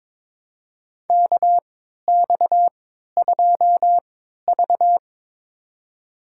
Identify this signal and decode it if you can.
Morse code, heard as a keyed tone, about 22 wpm, spaced as for 13 wpm, 700 Hz. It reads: KX2V
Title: USA Call Signs - 4 Characters - 3x Character Spacing - 22wpm